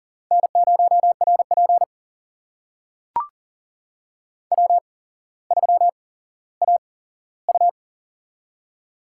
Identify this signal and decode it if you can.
Morse code, heard as a keyed tone, about 40 wpm, spaced as for 17 wpm, 700 Hz. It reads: N0RP W3AU
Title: USA Call Signs - 4 Characters - 8x Character Spacing - 40wpm